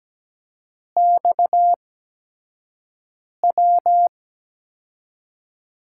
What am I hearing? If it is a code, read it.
Morse code, heard as a keyed tone, about 17 wpm, spaced as for 5 wpm, 700 Hz. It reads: XW
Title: USA Call Signs - 8x Character Spacing - 17wpm